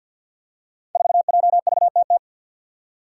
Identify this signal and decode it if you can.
Morse code, heard as a keyed tone, about 53 wpm, 700 Hz. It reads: 4JVTT
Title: Five Challenging Letter-Number Combinations 50wpm